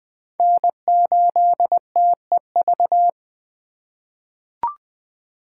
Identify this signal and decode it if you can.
Morse code, heard as a keyed tone, about 21 wpm, 700 Hz. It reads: N8TEV
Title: USA Call Signs - 3x Character Spacing - 20wpm